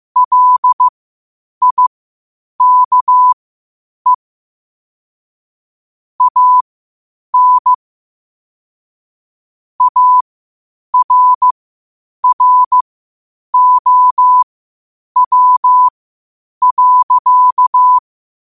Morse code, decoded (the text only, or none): LIKE AN ARROW.